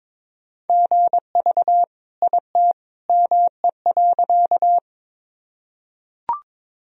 Morse code, heard as a keyed tone, about 21 wpm, 700 Hz. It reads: GV IT ME.
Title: Sentences from Top 200 Words and Abbreviations 22wpm